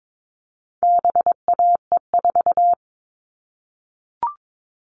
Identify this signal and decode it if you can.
Morse code, heard as a keyed tone, about 22 wpm, 700 Hz. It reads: BAE4
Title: All Cut Numbers - 4-Digit 22wpm